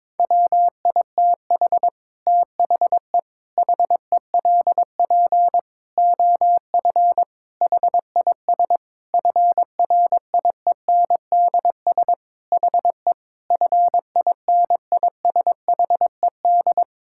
Morse code, heard as a keyed tone, about 22 wpm, 700 Hz. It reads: WITH THE HELP OF HIS FRIENDS HE FINISHED
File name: practice / 22wpm / Sentences with 4-Word Phrases - 22wpm